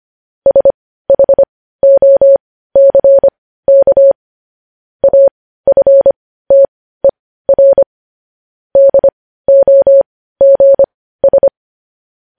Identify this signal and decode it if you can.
Morse code, heard as a keyed tone, about 25 wpm, spaced as for 15 wpm, 550 Hz. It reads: SHOCK AFTER DOGS
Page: News Headlines in Morse Code at 15 WPM